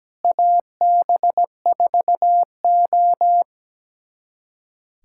Morse code, heard as a keyed tone, about 17 wpm, 700 Hz. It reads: AB4O
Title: USA Callsign DE Callsign - 4 Characters - 17wpm